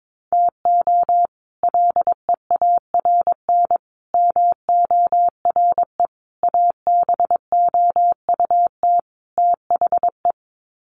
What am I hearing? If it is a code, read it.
Morse code, heard as a keyed tone, about 22 wpm, 700 Hz. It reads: TO LEARN MORE ABOUT THE
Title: Sets of 5 Words Encore2 22wpm